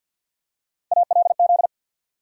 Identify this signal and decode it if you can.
Morse code, heard as a keyed tone, about 47 wpm, 700 Hz. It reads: ALZ
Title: USA Call Sign Postfixes - 3 Letters 50wpm